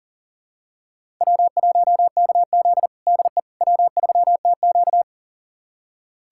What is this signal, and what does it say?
Morse code, heard as a keyed tone, about 40 wpm, 700 Hz. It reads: W1KZ DE W3TQ
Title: USA Callsign DE Callsign - 4 Characters - 40wpm